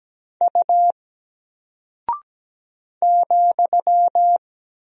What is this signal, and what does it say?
Morse code, heard as a keyed tone, about 17 wpm, 700 Hz. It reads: U ,
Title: Single Character (Letter:Number:Punctuation:Prosign) 17wpm